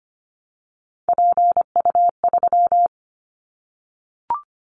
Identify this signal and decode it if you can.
Morse code, heard as a keyed tone, about 25 wpm, 700 Hz. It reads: PU3
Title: Callsign Prefixes for Top 100 DXCC - 25wpm